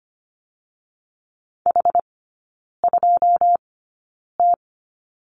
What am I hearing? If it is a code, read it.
Morse code, heard as a keyed tone, about 25 wpm, spaced as for 9 wpm, 690 Hz. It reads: H2T
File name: Lesson 14e - Number 2 - Three Characters 25wpm